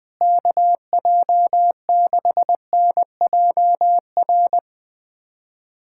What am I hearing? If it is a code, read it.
Morse code, heard as a keyed tone, about 21 wpm, 700 Hz. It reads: KJ6NJR
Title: USA Callsign DE Callsign - 6 Characters - 20wpm